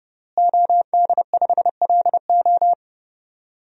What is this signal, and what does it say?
Morse code, heard as a keyed tone, about 31 wpm, 700 Hz. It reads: OD5LO
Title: DXCC-100 Call Signs - 5 Characters - Repeated 3x - 30wpm